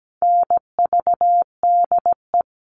Morse code, heard as a keyed tone, about 17 wpm, 700 Hz. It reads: NVDE
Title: All Cut Numbers - 4-Digit 17wpm